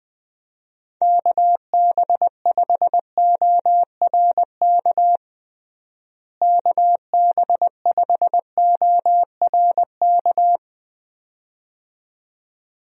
Morse code, heard as a keyed tone, about 20 wpm, 700 Hz. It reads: KB5ORK KB5ORK
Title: USA Call Signs - 6 Characters - Repeated 2x - 20wpm